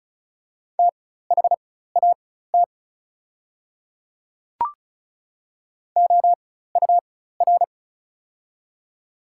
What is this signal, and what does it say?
Morse code, heard as a keyed tone, about 35 wpm, spaced as for 16 wpm, 700 Hz. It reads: THAT OUR
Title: 4x Character Spacing - Top 200 Words - 35wpm